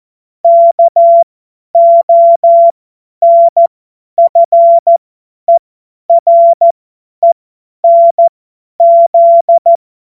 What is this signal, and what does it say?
Morse code, heard as a keyed tone, about 14 wpm, spaced as for 10 wpm, 680 Hz. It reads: KONFERENZ
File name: Morsekurs / Witze / Witz 17 Juni 2025 50 F70